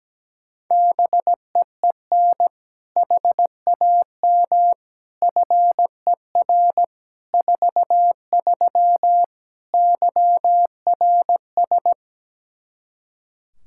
Morse code, heard as a keyed tone, about 17 wpm, 700 Hz. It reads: BEEN HAM FER 43 YRS
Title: Practice QSO - Exchange 3 - More Info - 17wpm